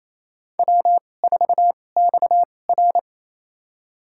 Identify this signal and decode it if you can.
Morse code, heard as a keyed tone, about 28 wpm, spaced as for 20 wpm, 700 Hz. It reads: W4XR
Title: USA Call Signs - 2x Character Spacing - 28wpm